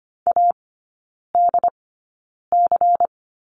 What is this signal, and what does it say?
Morse code, heard as a keyed tone, about 25 wpm, spaced as for 9 wpm, 700 Hz. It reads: ADC
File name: course / 25:10wpm / 093 Lesson 16e - Letter c - Three Characters 25wpm